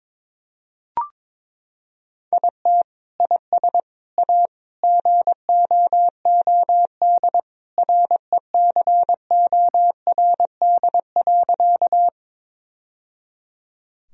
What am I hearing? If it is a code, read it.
Morse code, heard as a keyed tone, about 22 wpm, 700 Hz. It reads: IT IS A GOOD RECORD.
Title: Sentences from Top 1000 Words - Review Part 3 - 22wpm